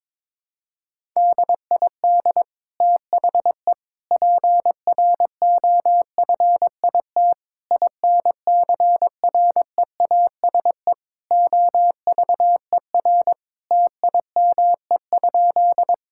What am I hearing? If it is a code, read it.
Morse code, heard as a keyed tone, about 22 wpm, 700 Hz. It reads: DID THE PROFIT INCREASE OVER TIME?
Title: Sentences from Top 1000 Words - Review Part 5 - 22wpm